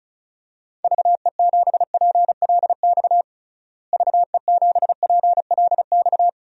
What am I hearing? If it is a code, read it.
Morse code, heard as a keyed tone, about 36 wpm, 700 Hz. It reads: VE7PLX VE7PLX
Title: DXCC-100 Call Signs - 6 Characters - Repeated 2x - 35wpm